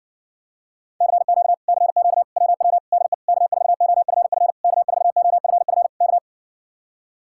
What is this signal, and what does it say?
Morse code, heard as a keyed tone, about 60 wpm, 700 Hz. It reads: CQ CQ WW DE K4OFF K4OFF K